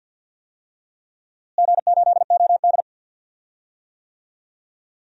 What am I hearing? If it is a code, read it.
Morse code, heard as a keyed tone, about 47 wpm, 700 Hz. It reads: G8OD
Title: DXCC-100 Call Signs - 4 Characters - Repeated 3x - 50wpm